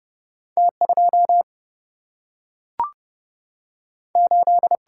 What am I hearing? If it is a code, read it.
Morse code, heard as a keyed tone, about 30 wpm, 700 Hz. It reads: T 2 8
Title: Single-Letter Single-Number 30wpm